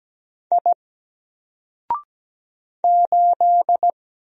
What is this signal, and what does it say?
Morse code, heard as a keyed tone, about 17 wpm, 700 Hz. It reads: I 8
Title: Single Letter-Number Alt 17wpm